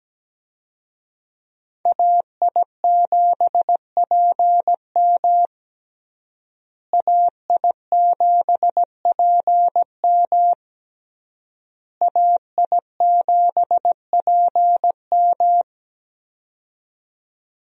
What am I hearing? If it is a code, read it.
Morse code, heard as a keyed tone, about 17 wpm, 700 Hz. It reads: AI7PM AI7PM AI7PM
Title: USA Call Signs - 5 Characters - Repeated 3x - 17wpm